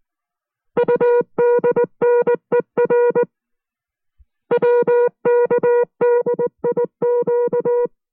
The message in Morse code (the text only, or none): UDNER WKDIQ